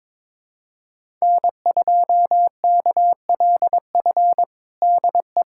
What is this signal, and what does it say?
Morse code, heard as a keyed tone, about 22 wpm, 700 Hz. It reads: N2KLF DE
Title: USA Callsign DE Callsign - 5 Characters - 22wpm